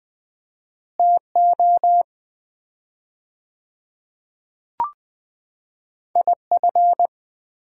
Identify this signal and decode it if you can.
Morse code, heard as a keyed tone, about 21 wpm, 700 Hz. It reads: TO IF
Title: 4x Word-Spacing - Sets of 5 Words - 20wpm